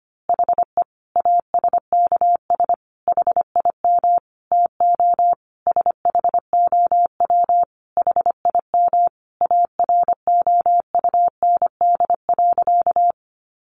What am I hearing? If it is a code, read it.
Morse code, heard as a keyed tone, about 25 wpm, 700 Hz. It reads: HE ASKS HIM TO SHOW HIM AROUND.
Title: Easy Phrases 25wpm